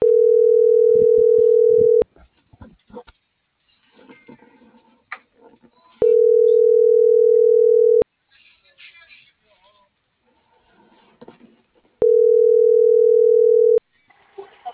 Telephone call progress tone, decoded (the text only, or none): ringback tone